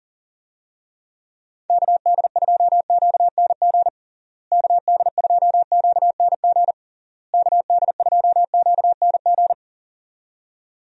Morse code, heard as a keyed tone, about 40 wpm, 700 Hz. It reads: KD2QNG KD2QNG KD2QNG
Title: USA Call Signs - 6 Characters - Repeated 3x - 40wpm